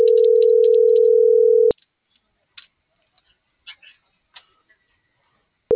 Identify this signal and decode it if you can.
Ringback tone (North America), 440 + 480 Hz.